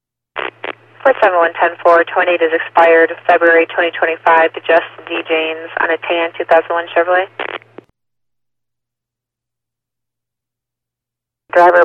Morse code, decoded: none

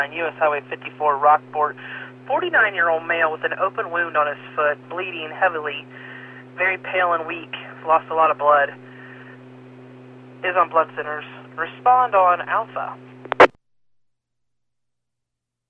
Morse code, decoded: none

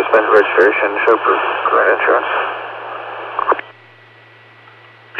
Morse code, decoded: none

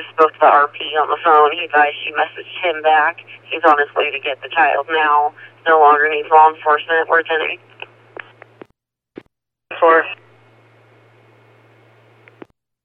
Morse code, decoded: none

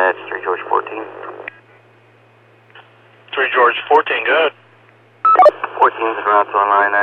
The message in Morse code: none

